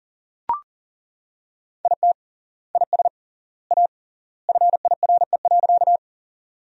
Morse code, heard as a keyed tone, about 40 wpm, 700 Hz. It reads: IT IS A FIRE.
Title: 3x Word-Spacing - Sentences from Top 400 Words - 40wpm